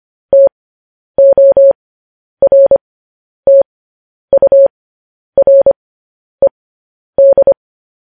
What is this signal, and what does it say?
Morse code, heard as a keyed tone, about 25 wpm, spaced as for 10 wpm, 550 Hz. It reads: TORTURED
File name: podcast / News headlines-1x-25x10